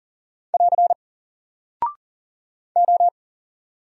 Morse code, heard as a keyed tone, about 40 wpm, 700 Hz. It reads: <AR> O
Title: Single Character (Letter:Number:Punctuation:Prosign) Alt 40wpm